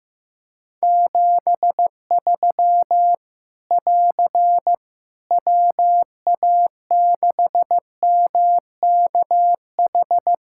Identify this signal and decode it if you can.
Morse code, heard as a keyed tone, about 15 wpm, 700 Hz. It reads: 73 <AR> WA6MKH
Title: Practice QSO - Exchange 4 - Ending - 15wpm